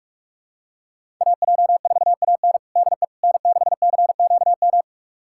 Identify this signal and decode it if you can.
Morse code, heard as a keyed tone, about 46 wpm, 700 Hz. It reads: AJ4AN DE N6CQM
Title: USA Callsign DE Callsign - 5 Characters - 45wpm